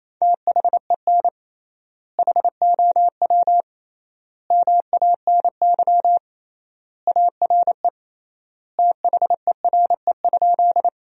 Morse code, heard as a keyed tone, about 28 wpm, 700 Hz. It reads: THEN HOW MANY ARE THERE?
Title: 3x Word-Spacing - Sentences from Top 100 Words - 28wpm